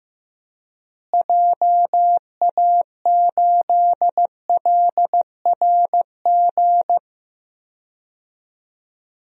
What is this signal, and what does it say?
Morse code, heard as a keyed tone, about 15 wpm, 700 Hz. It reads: JA8LRG
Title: DXCC-100 Call Signs - 6 Characters - Repeated 3x - 15wpm